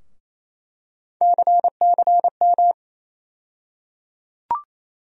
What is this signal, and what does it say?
Morse code, heard as a keyed tone, about 28 wpm, 700 Hz. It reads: CCM
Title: USA Call Sign Postfixes - 3 Letters 28wpm